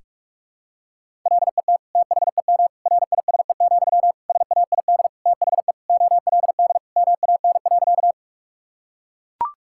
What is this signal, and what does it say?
Morse code, heard as a keyed tone, about 41 wpm, 700 Hz. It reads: LET THEM RISE, SAID THE OLD MAN.
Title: Sentences from Top 1000 Words - Review Part 6 - 45wpm